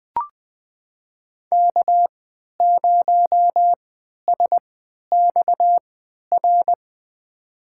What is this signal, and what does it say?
Morse code, heard as a keyed tone, about 20 wpm, spaced as for 11 wpm, 700 Hz. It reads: K0SXR
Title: USA Call Signs - 5 Characters - 3x Character Spacing - 20wpm